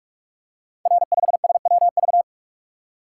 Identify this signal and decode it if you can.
Morse code, heard as a keyed tone, about 44 wpm, 700 Hz. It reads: R5SWV
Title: Code Groups - 5 Letter:Numbers - 45wpm